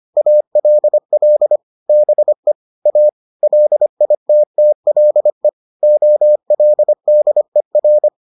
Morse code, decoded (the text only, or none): ALL BE A LITTLE OLDER